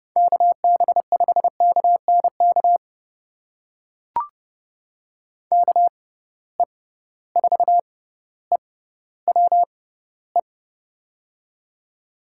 Morse code, heard as a keyed tone, about 31 wpm, spaced as for 16 wpm, 700 Hz. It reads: KB5KNK KE4EWE
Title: USA Call Signs - 6 Characters - 6x Character Spacing - 30wpm